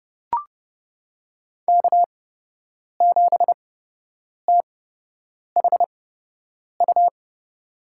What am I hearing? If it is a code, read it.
Morse code, heard as a keyed tone, about 30 wpm, spaced as for 8 wpm, 700 Hz. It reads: K7THU
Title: USA Call Signs - 5 Characters - 8x Character Spacing - 30wpm